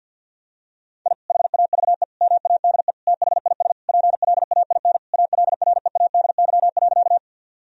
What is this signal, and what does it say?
Morse code, heard as a keyed tone, about 50 wpm, 700 Hz. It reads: I HAVE MADE THIS PLAIN ALREADY.
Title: Sentences from Top 1000 Words - Review Part 2 - 50wpm